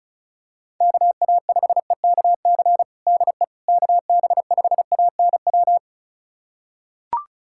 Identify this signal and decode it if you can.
Morse code, heard as a keyed tone, about 34 wpm, 700 Hz. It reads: KA5EKC DE KB5ANW
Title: USA Callsign DE Callsign - 6 Characters - 35wpm